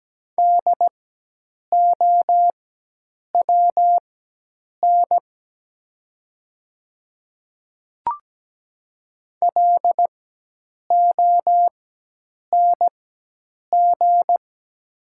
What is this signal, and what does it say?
Morse code, heard as a keyed tone, about 17 wpm, spaced as for 8 wpm, 700 Hz. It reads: DOWN LONG
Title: 4x Character Spacing - Top 100 Words - 17wpm